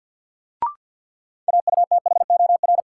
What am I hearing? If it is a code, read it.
Morse code, heard as a keyed tone, about 47 wpm, 700 Hz. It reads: AUTHOR
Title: Sets of 4 Words - New Word Drill - 50wpm